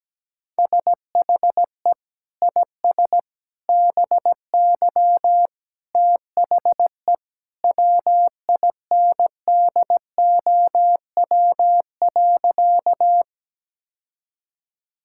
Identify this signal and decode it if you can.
Morse code, heard as a keyed tone, about 17 wpm, 700 Hz. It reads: SHE IS BY THE WINDOW.
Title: Sentences from Top 1600 Words 17wpm